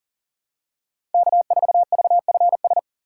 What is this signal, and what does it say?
Morse code, heard as a keyed tone, about 38 wpm, 700 Hz. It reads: K4VFS
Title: USA Call Signs - 6x Character Spacing - 40wpm